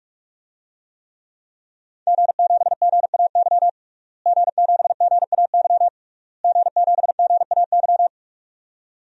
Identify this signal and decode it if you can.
Morse code, heard as a keyed tone, about 46 wpm, 700 Hz. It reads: G7GAY G7GAY G7GAY
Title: DXCC-100 Call Signs - 5 Characters - Repeated 3x - 45wpm